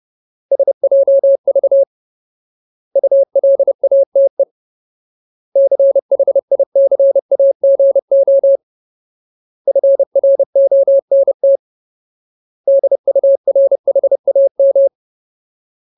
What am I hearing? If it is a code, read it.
Morse code, heard as a keyed tone, about 30 wpm, 550 Hz. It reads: SJV ULATE CHICAGO FRONT DURHAM
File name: Ultimate Word List / 30WPM Ultimate